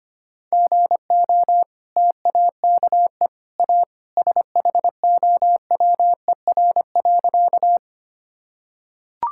Morse code, heard as a keyed tone, about 25 wpm, 700 Hz. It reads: GO TAKE A SHOWER.
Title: Sentences from Top 1600 Words 25wpm